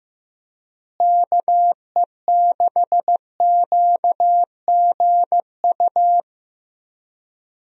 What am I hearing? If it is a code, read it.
Morse code, heard as a keyed tone, about 15 wpm, 700 Hz. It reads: KE6QGU
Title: USA Call Signs - 6 Characters - Repeated 3x - 15wpm